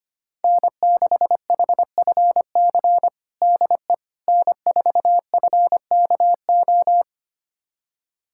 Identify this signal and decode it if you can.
Morse code, heard as a keyed tone, about 25 wpm, 700 Hz. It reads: N6HFC DE N4FKO
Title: USA Callsign DE Callsign - 5 Characters - 25wpm